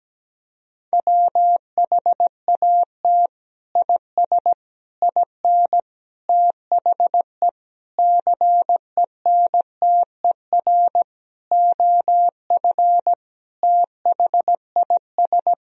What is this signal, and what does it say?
Morse code, heard as a keyed tone, about 17 wpm, 700 Hz. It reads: WHAT IS IN THE CENTER OF THIS
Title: Sentences from Top 1000 Words - Review Part 2 - 17wpm